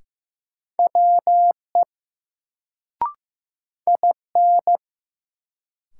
Morse code, heard as a keyed tone, about 15 wpm, 700 Hz. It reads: WE IN
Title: Sentences from Top 100 Words - New Word Drill - 15wpm